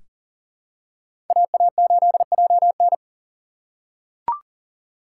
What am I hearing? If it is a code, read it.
Morse code, heard as a keyed tone, about 40 wpm, 700 Hz. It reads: AA8JN
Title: USA Call Signs - 5 Characters - 2x Character Spacing - 40wpm